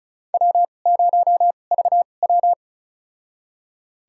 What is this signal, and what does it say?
Morse code, heard as a keyed tone, about 36 wpm, spaced as for 25 wpm, 690 Hz. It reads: W0VW
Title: USA Call Signs - 4 Characters - 2x Character Spacing - 35wpm